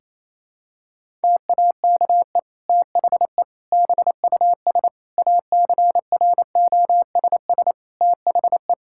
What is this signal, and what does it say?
Morse code, heard as a keyed tone, about 28 wpm, 700 Hz. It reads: TAKE THE BUS ACROSS THE
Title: Sentences from Top 1000 Words - Review Part 6 - 28wpm